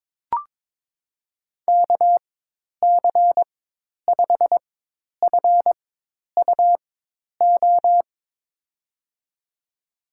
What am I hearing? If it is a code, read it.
Morse code, heard as a keyed tone, about 22 wpm, spaced as for 10 wpm, 700 Hz. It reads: KC5FUO
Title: USA Call Signs - 4x Character Spacing - 22wpm